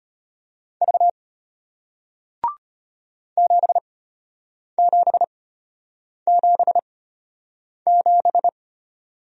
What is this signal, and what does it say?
Morse code, heard as a keyed tone, about 34 wpm, spaced as for 8 wpm, 690 Hz. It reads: V 7777